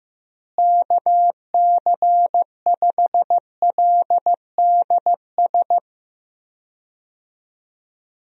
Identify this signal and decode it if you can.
Morse code, heard as a keyed tone, about 15 wpm, 700 Hz. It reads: KC5LDS
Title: USA Call Signs - 6 Characters - Repeated 3x - 15wpm